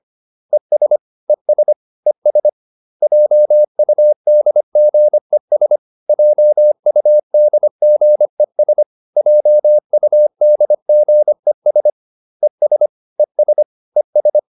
Morse code, heard as a keyed tone, about 25 wpm, 600 Hz. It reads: ES ES ES JUDGES JUDGES JUDGES ES ES ES